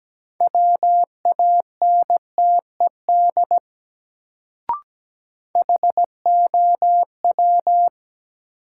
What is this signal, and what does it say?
Morse code, heard as a keyed tone, about 17 wpm, 700 Hz. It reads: WANTED HOW
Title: Sets of 2 Words - New Word Drill - 17wpm